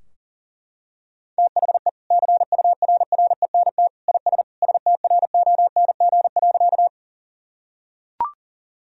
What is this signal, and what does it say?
Morse code, heard as a keyed tone, about 40 wpm, 700 Hz. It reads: THE CURRENT IS STRONG.
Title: Sentences from Top 1000 Words - Review Part 4 - 40wpm